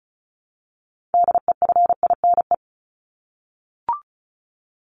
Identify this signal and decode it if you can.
Morse code, heard as a keyed tone, about 34 wpm, 700 Hz. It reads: DEFINE
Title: ED Suffix - Top 100 Words with Base Words 35wpm